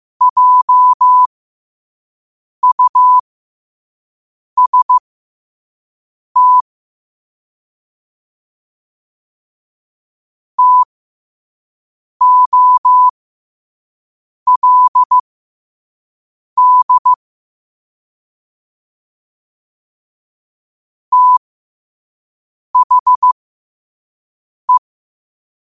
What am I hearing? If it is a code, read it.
Morse code, heard as a keyed tone, about 15 wpm, spaced as for 5 wpm, 1000 Hz. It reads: JUST TOLD THE
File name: qotdfiles / 5wpm